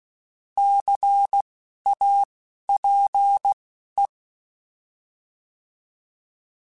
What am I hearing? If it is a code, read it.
Morse code, heard as a keyed tone, about 16 wpm, spaced as for 12 wpm, 800 Hz. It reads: CAPE
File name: Alphabetical / Alphabetical mp3s / Cs @10